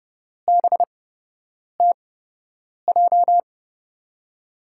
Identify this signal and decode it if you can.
Morse code, heard as a keyed tone, about 30 wpm, spaced as for 8 wpm, 700 Hz. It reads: BTJ
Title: USA Call Signs - 6 Characters - 8x Character Spacing - 30wpm